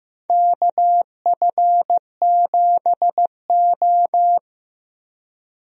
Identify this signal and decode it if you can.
Morse code, heard as a keyed tone, about 15 wpm, 700 Hz. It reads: KF7O
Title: USA Call Signs - 4 Characters - Repeated 3x - 15wpm